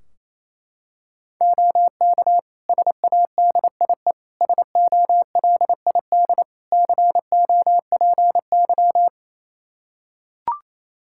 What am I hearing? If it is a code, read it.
Morse code, heard as a keyed tone, about 28 wpm, 700 Hz. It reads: OK SADIE SOLID COPY
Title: Practice QSO - Exchange 3 - More Info - 28wpm